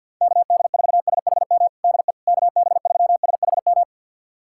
Morse code, heard as a keyed tone, about 52 wpm, 700 Hz. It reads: KD4SHM DE KB3SHM
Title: USA Callsign DE Callsign - 6 Characters - 50wpm